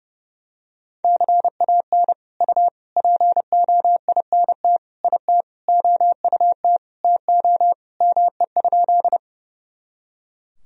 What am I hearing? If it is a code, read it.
Morse code, heard as a keyed tone, about 32 wpm, 700 Hz. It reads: CAN U POINT IT OUT TO ME?
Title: Sentences from Top 200 Words and Abbreviations 30wpm